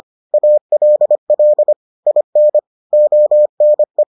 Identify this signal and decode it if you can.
Morse code, heard as a keyed tone, about 25 wpm, 600 Hz. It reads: ALL IN ONE